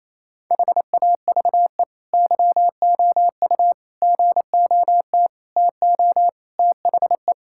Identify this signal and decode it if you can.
Morse code, heard as a keyed tone, about 28 wpm, 700 Hz. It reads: HAVE YOU GOT TO THE
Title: Sentences from Top 1000 Words - Review Part 6 - 28wpm